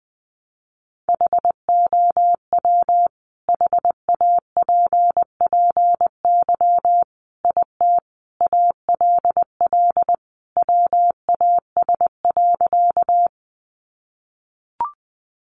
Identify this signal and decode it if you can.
Morse code, heard as a keyed tone, about 20 wpm, 700 Hz. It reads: HOW HAPPY IT ALL WAS.
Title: Sentences from Top 1000 Words 20wpm